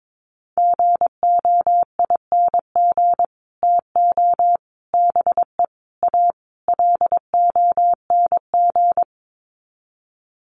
Morse code, heard as a keyed tone, about 22 wpm, 700 Hz. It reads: GOING TO BE A LONG
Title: Sets of 5 Words Encore2 22wpm